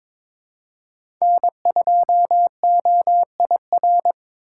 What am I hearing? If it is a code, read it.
Morse code, heard as a keyed tone, about 22 wpm, 700 Hz. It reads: N2OIR
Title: USA Call Signs - 5 Characters - 4x Character Spacing - 22wpm